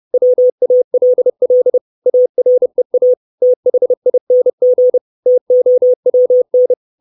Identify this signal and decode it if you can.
Morse code, heard as a keyed tone, about 30 wpm, 500 Hz. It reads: WALL AREA THING TOWN
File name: 632 Words 30WPM